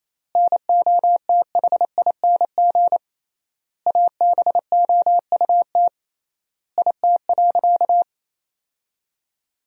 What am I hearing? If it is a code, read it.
Morse code, heard as a keyed tone, about 28 wpm, 700 Hz. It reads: NOTHING ABOUT IT.
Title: 3x Word-Spacing - Sentences from Top 500 Words - 28wpm